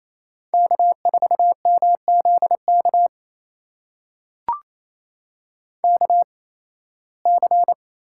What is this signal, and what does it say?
Morse code, heard as a keyed tone, about 28 wpm, spaced as for 18 wpm, 700 Hz. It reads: K4MZK KC
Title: USA Call Signs - 8x Character Spacing - 28wpm